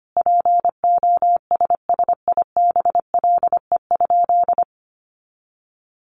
Morse code, heard as a keyed tone, about 25 wpm, 700 Hz. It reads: POSSIBLE?
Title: Sentences from Top 500 Words 25wpm